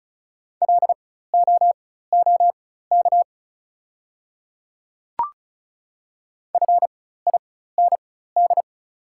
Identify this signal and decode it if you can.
Morse code, heard as a keyed tone, about 36 wpm, spaced as for 16 wpm, 700 Hz. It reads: LOOK FIND
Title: 4x Character Spacing - Top 100 Words - 35wpm